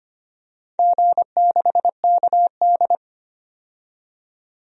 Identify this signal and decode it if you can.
Morse code, heard as a keyed tone, about 25 wpm, 700 Hz. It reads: G6KD
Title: DXCC-100 Call Signs - 4 Characters - 25wpm